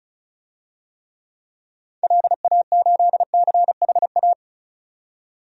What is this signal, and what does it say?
Morse code, heard as a keyed tone, about 36 wpm, 700 Hz. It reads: LA8CHA
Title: DXCC-100 Call Signs - 6 Characters - 35wpm